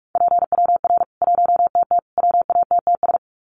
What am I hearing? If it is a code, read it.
Morse code, heard as a keyed tone, about 44 wpm, 700 Hz. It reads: PWR 1TT WATTS
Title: Common QSO Phrases 45wpm